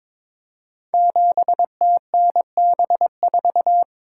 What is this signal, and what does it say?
Morse code, heard as a keyed tone, about 22 wpm, 700 Hz. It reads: 7TNB4
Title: Code Groups - 5 Letter:Numbers - 22wpm